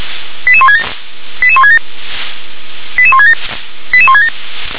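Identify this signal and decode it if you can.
Five-tone selective call (ZVEI), heard as quick runs of five tones, tones of about 70 ms. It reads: 88167 88167 88167 88167
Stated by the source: ZVEI five-tone selective call received on an SDR.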